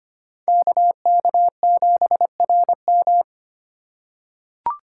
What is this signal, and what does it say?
Morse code, heard as a keyed tone, about 25 wpm, 700 Hz. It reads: KK7RM
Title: USA Call Signs - 5 Characters - 2x Character Spacing - 25wpm